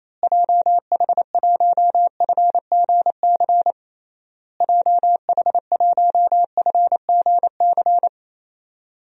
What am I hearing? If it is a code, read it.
Morse code, heard as a keyed tone, about 28 wpm, 700 Hz. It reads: JH1FGC JH1FGC
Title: DXCC-100 Call Signs - 6 Characters - Repeated 3x - 28wpm